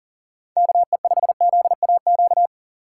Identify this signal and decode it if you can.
Morse code, heard as a keyed tone, about 40 wpm, 690 Hz. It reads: KE5ZAQ
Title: USA Call Signs - 3x Character Spacing - 40wpm